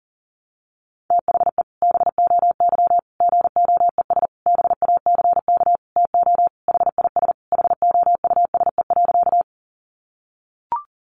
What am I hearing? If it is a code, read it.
Morse code, heard as a keyed tone, about 40 wpm, 700 Hz. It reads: THE BOY GOES BACK TO HIS HOUSE.
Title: Additional Easy Phrases 40wpm